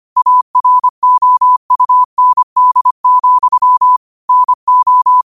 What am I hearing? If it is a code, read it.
Morse code, heard as a keyed tone, about 25 wpm, 1000 Hz. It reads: AROUND, NO